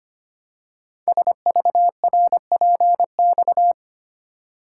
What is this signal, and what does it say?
Morse code, heard as a keyed tone, about 25 wpm, 700 Hz. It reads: SVRPX
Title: Code Groups - 5 Letters - 25wpm